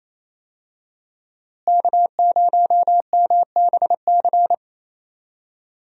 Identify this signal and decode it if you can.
Morse code, heard as a keyed tone, about 28 wpm, 700 Hz. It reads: K0MBC
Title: USA Call Signs - 5 Characters - Repeated 3x - 28wpm